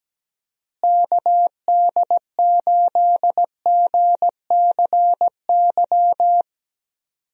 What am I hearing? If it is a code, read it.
Morse code, heard as a keyed tone, about 17 wpm, 700 Hz. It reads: KD8GCY
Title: USA Call Signs - 6 Characters - Repeated 3x - 17wpm